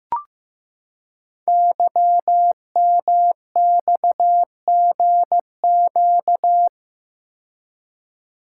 Morse code, heard as a keyed tone, about 15 wpm, 700 Hz. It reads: YMXGQ
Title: Code Groups - 5 Letters - 15wpm